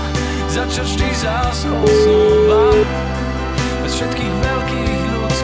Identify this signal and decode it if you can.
Ringback tone (Europe, 425 Hz), with noise.